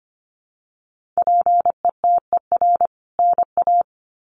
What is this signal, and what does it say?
Morse code, heard as a keyed tone, about 24 wpm, 700 Hz. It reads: PETER NA
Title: CWT Exchange 25wpm